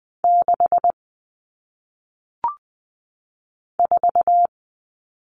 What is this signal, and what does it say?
Morse code, heard as a keyed tone, about 20 wpm, 700 Hz. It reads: 6 4
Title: Numbers - 1-Digit 20wpm